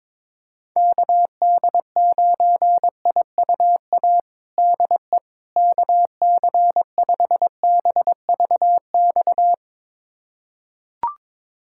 Morse code, heard as a keyed tone, about 22 wpm, 700 Hz. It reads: KD9IUA DE KC5BVX
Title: USA Callsign DE Callsign 22wpm